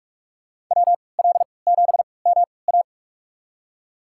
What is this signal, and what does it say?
Morse code, heard as a keyed tone, about 47 wpm, spaced as for 26 wpm, 700 Hz. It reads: WL7MA
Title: USA Call Signs - 3x Character Spacing - 45wpm